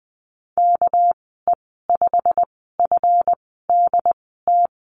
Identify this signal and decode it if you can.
Morse code, heard as a keyed tone, about 20 wpm, spaced as for 15 wpm, 700 Hz. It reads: KE5FDT
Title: 213 Lesson 34g - Number 8 - Call Signs 20wpm